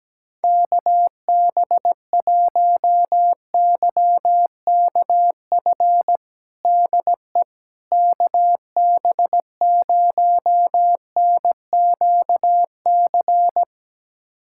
Morse code, heard as a keyed tone, about 17 wpm, 700 Hz. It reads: KB1YKF DE KB0NQC
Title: USA Callsign DE Callsign - 6 Characters - 17wpm